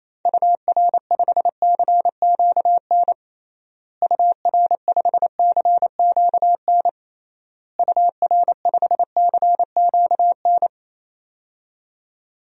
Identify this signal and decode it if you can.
Morse code, heard as a keyed tone, about 28 wpm, 700 Hz. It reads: UR5CQN UR5CQN UR5CQN
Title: DXCC-100 Call Signs - 6 Characters - Repeated 3x - 28wpm